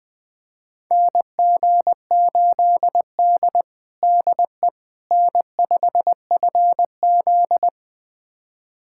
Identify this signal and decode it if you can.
Morse code, heard as a keyed tone, about 20 wpm, 700 Hz. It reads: NG8D DE N5FZ
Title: USA Callsign DE Callsign - 4 Characters - 20wpm